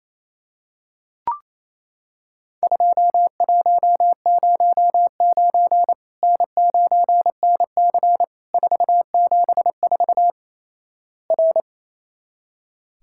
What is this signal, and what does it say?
Morse code, heard as a keyed tone, about 28 wpm, 700 Hz. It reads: 2109 N9NC 474 R
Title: Worked All Europe DX Contest - QTC Exchange - 28wpm